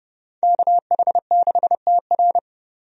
Morse code, heard as a keyed tone, about 32 wpm, 700 Hz. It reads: KH6TR
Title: DXCC-100 Call Signs - 5 Characters - 30wpm